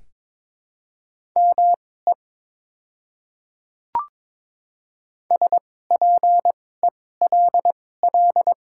Morse code, heard as a keyed tone, about 23 wpm, spaced as for 16 wpm, 700 Hz. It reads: ME SPELL
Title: 2x Character Spacing - Top 500 Words - 22wpm